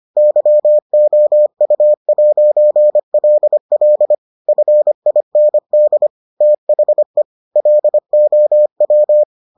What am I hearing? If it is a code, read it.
Morse code, heard as a keyed tone, about 25 wpm, 600 Hz. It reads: YOU'LL FIND THE LOW